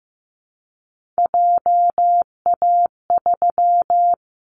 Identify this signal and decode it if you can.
Morse code, heard as a keyed tone, about 15 wpm, 700 Hz. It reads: JA3
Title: Callsign Prefixes for Top 50 DXCC - 15wpm